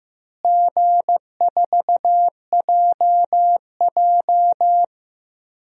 Morse code, heard as a keyed tone, about 15 wpm, 700 Hz. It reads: G4JJ
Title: DXCC-100 Call Signs - 4 Characters - Repeated 3x - 15wpm